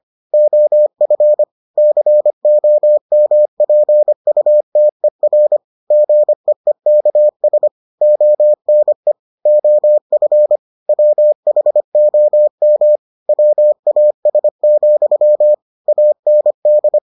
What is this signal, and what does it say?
Morse code, heard as a keyed tone, about 25 wpm, 600 Hz. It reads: OF COMPUTER GEEKS ONE OF WHOM WAS, AND